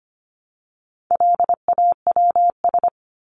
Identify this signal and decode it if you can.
Morse code, heard as a keyed tone, about 25 wpm, 700 Hz. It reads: LAWS
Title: ES:S Suffix - Top 100 Words with Base Words 25wpm